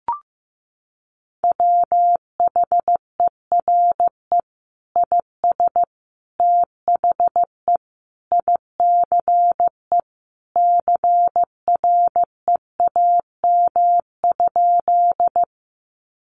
Morse code, heard as a keyed tone, about 15 wpm, 700 Hz. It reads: WHERE IS THE ICE CREAM?
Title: Sentences from Top 1500 Words 15wpm